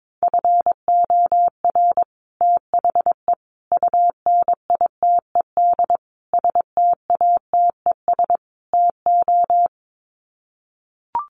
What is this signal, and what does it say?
Morse code, heard as a keyed tone, about 22 wpm, 700 Hz. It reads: FOR THE UNITED STATES TO
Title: Sets of 5 Words Encore 22wpm